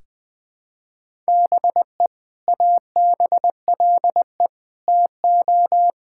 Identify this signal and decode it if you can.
Morse code, heard as a keyed tone, about 21 wpm, 700 Hz. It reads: BE ABLE TO
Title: Sentences with 2-Word Phrases - 20wpm